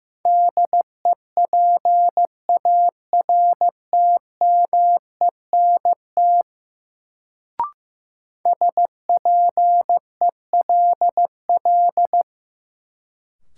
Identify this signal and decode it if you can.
Morse code, heard as a keyed tone, about 15 wpm, 700 Hz. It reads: DEPARTMENT SPELL
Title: Sentences from Top 1200 Words - New Word Drill Alt - 15wpm